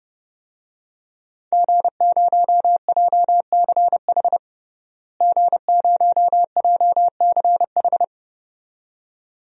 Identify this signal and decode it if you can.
Morse code, heard as a keyed tone, about 32 wpm, 700 Hz. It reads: G0JCH G0JCH
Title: DXCC-100 Call Signs - 5 Characters - Repeated 2x - 30wpm